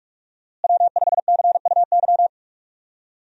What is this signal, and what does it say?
Morse code, heard as a keyed tone, about 44 wpm, 700 Hz. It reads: W5CUY
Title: USA Call Signs - 5 Characters - Repeated 2x - 45wpm